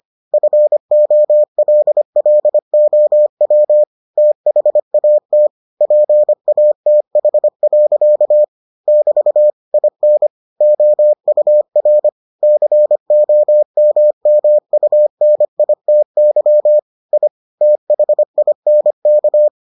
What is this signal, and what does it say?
Morse code, heard as a keyed tone, about 25 wpm, 600 Hz. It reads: FOLLOW THAT PATH. <BT> IN OUR COMMUNITY I THINK